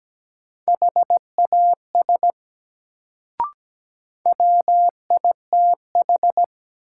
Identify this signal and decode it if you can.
Morse code, heard as a keyed tone, about 17 wpm, 700 Hz. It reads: HAS WITH
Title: Sentences from Top 100 Words - New Word Drill - 17wpm